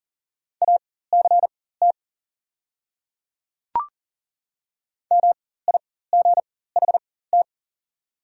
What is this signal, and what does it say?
Morse code, heard as a keyed tone, about 40 wpm, spaced as for 19 wpm, 690 Hz. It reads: ACT MIGHT
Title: 4x Character Spacing - Top 300 Words - 40wpm